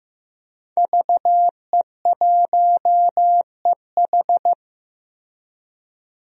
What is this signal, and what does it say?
Morse code, heard as a keyed tone, about 15 wpm, 700 Hz. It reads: VE1EH
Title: DXCC-100 Call Signs - 5 Characters - Repeated 2x - 15wpm